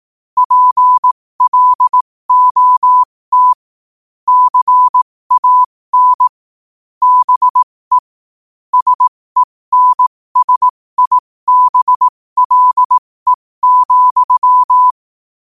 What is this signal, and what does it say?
Morse code, heard as a keyed tone, about 18 wpm, spaced as for 15 wpm, 1000 Hz. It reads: PLOT CAN BE SENSIBLE,